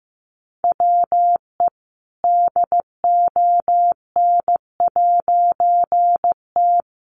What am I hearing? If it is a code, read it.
Morse code, heard as a keyed tone, about 15 wpm, 700 Hz. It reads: WE DON'T
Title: A Sound Of Thunder 15wpm